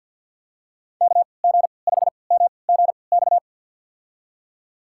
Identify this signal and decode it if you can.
Morse code, heard as a keyed tone, about 52 wpm, spaced as for 29 wpm, 700 Hz. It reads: KG5MGX
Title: USA Call Signs - 3x Character Spacing - 50wpm